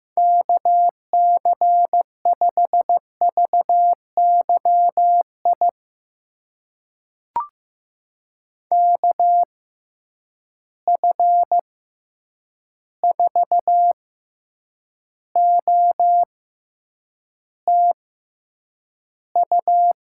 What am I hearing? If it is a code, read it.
Morse code, heard as a keyed tone, about 15 wpm, spaced as for 8 wpm, 700 Hz. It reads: KC5VYI KF4OTU
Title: USA Call Signs - 6 Characters - 6x Character Spacing - 15wpm